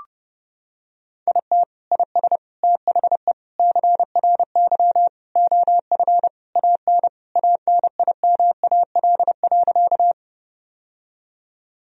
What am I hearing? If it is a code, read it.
Morse code, heard as a keyed tone, about 30 wpm, 700 Hz. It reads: IT IS THE CRY OF AN ANIMAL.
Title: Sentences from Top 1000 Words - Review Part 2 - 30wpm